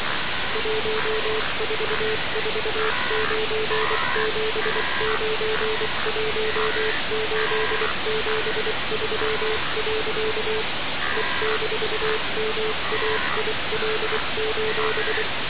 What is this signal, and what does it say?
Morse code, heard as a keyed tone, about 24 wpm, 440 Hz, with noise. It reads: 1449791873. E-MAIL:
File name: radiocap / REC z